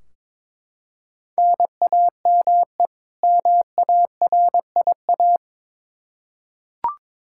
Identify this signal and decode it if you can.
Morse code, heard as a keyed tone, about 22 wpm, 700 Hz. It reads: NAME MARIA
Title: Practice QSO - Exchange 1 - Beginning - 22wpm